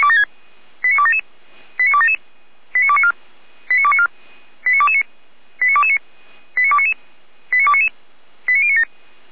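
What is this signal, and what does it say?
Five-tone selective call (ZVEI), heard as quick runs of five tones, tones of about 70 ms. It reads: 89277 89277 89294 89294 89229 89229 89299 89299 89097
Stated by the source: ZVEI five-tone selective call received by SDR.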